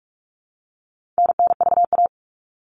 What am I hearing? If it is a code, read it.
Morse code, heard as a keyed tone, about 44 wpm, 700 Hz. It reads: NNVA
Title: All Cut Numbers - 4-Digit 45wpm